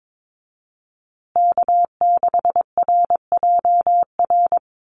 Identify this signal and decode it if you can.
Morse code, heard as a keyed tone, about 22 wpm, 700 Hz. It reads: K6RJR
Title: USA Call Signs Repeated 2x 22wpm